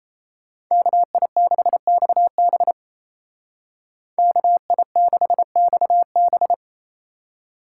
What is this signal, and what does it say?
Morse code, heard as a keyed tone, about 36 wpm, 700 Hz. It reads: KI6XB KI6XB